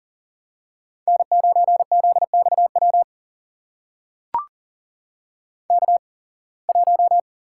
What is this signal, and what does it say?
Morse code, heard as a keyed tone, about 40 wpm, spaced as for 26 wpm, 700 Hz. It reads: N9ZXW K1
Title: USA Call Signs - 5 Characters - 8x Character Spacing - 40wpm